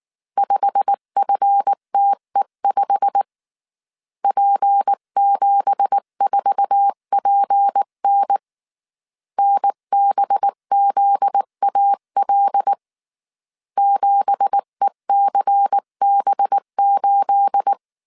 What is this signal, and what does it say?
Morse code, heard as a keyed tone, about 19 wpm, 800 Hz. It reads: 5FTE5 P74PN NBZAL 7ECB8